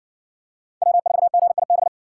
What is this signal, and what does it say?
Morse code, heard as a keyed tone, about 57 wpm, 700 Hz. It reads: W4GIB